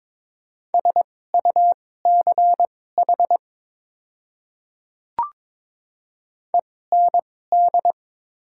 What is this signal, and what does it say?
Morse code, heard as a keyed tone, about 21 wpm, spaced as for 16 wpm, 700 Hz. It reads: SUCH END
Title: 2x Character Spacing - Top 200 Words - 22wpm